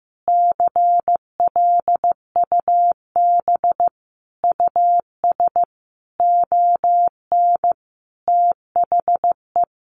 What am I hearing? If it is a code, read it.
Morse code, heard as a keyed tone, about 15 wpm, 700 Hz. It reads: CLUB US ON THE
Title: Sentences from Top 1200 Words 15wpm